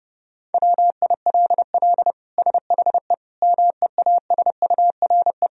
Morse code, heard as a keyed tone, about 30 wpm, 700 Hz. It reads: WILL SHE MEASURE
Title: Sentences from Top 1000 Words - Review Part 2 - 30wpm